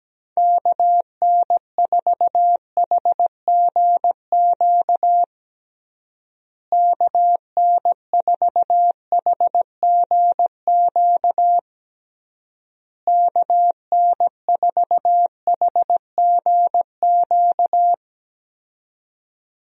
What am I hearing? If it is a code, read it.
Morse code, heard as a keyed tone, about 17 wpm, 700 Hz. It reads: KN4HGQ KN4HGQ KN4HGQ
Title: USA Call Signs - 6 Characters - Repeated 3x - 17wpm